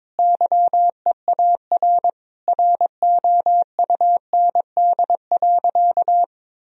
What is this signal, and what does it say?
Morse code, heard as a keyed tone, about 22 wpm, 700 Hz. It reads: YEAR ROUND.
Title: Sentences from Top 1000 Words - Review Part 1 - 22wpm